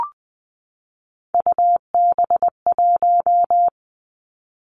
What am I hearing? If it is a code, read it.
Morse code, heard as a keyed tone, about 20 wpm, 700 Hz. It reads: UB1
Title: Three Challenging Letter-Number Combinations 20wpm